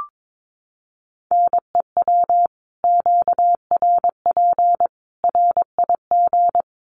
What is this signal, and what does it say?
Morse code, heard as a keyed tone, about 22 wpm, 700 Hz. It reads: NEW QRP RIG
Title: Common QSO Phrases 22wpm